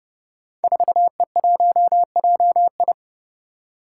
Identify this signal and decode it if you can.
Morse code, heard as a keyed tone, about 29 wpm, 700 Hz. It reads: 4E1JI
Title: Five Challenging Letter-Number Combinations 30wpm